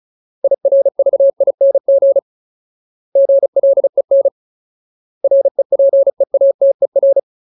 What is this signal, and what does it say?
Morse code, heard as a keyed tone, about 34 wpm, 550 Hz. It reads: IRVING GLEN REPEATER
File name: Ultimate Word List / 35WPM Ultimate